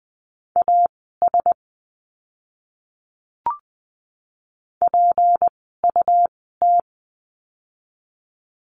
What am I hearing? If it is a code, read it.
Morse code, heard as a keyed tone, about 20 wpm, spaced as for 14 wpm, 700 Hz. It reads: AS PUT
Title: 206 Lesson 33f - Letter j - Words 20wpm